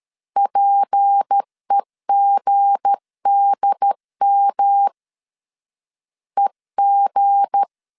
Morse code, heard as a keyed tone, about 13 wpm, 800 Hz. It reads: PEGDM EG
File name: LEZIONI / 7 GF9Z / 7 GF 12wpm 2122